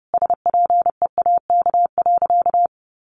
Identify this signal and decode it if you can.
Morse code, heard as a keyed tone, about 28 wpm, 700 Hz. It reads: SPEAK.
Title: Sentences from Top 1300 Words 30wpm